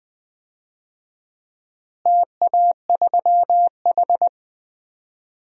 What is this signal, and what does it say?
Morse code, heard as a keyed tone, about 20 wpm, 700 Hz. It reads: TA3H
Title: DXCC-100 Call Signs - 4 Characters - Repeated 2x - 20wpm